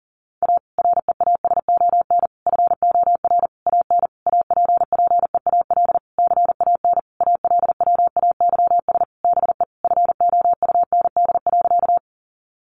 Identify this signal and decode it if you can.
Morse code, heard as a keyed tone, about 40 wpm, 700 Hz. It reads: A REASON FOR AN APPEAL CAN ALWAYS BE FOUND.
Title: Sentences from Top 1300 Words 40wpm